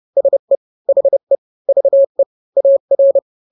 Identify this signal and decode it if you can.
Morse code, heard as a keyed tone, about 30 wpm, 550 Hz. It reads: SE HE VE AR
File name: n-grams / 2-grams 30 25 WPM